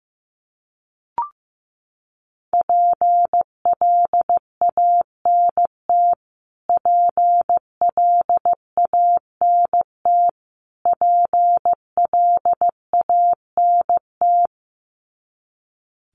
Morse code, heard as a keyed tone, about 15 wpm, 700 Hz. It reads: PLANT PLANT PLANT
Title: Top 500 Words Repeated 15wpm